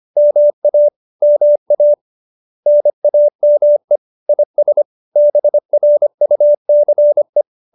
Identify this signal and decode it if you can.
Morse code, heard as a keyed tone, about 25 wpm, 590 Hz. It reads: MA MA NAME IS BRUCE